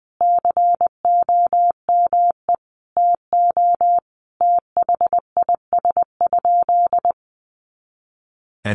Morse code, heard as a keyed tone, about 20 wpm, 700 Hz. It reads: COME TO THIS?
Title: Sentences from Top 100 Words 20wpm